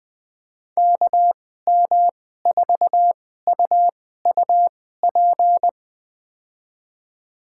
Morse code, heard as a keyed tone, about 21 wpm, spaced as for 15 wpm, 700 Hz. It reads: KM4UUP
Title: USA Call Signs - 2x Character Spacing - 20wpm